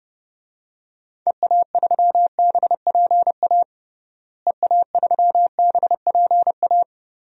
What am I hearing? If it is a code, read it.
Morse code, heard as a keyed tone, about 30 wpm, 700 Hz. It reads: EA3BPA EA3BPA
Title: DXCC-100 Call Signs - 6 Characters - Repeated 2x - 30wpm